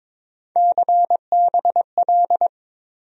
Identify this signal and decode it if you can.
Morse code, heard as a keyed tone, about 22 wpm, 700 Hz. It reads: CBL
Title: USA Call Sign Postfixes - 3 Letters 22wpm